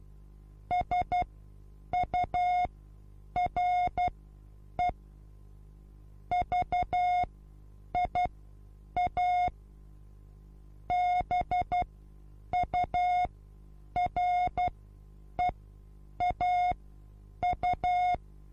Morse code, heard as a keyed tone, about 12 wpm, spaced as for 8 wpm, 720 Hz. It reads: SURE VIA BUREAU
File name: Onlinelehrgang / morsen / DJ4UF Morsen23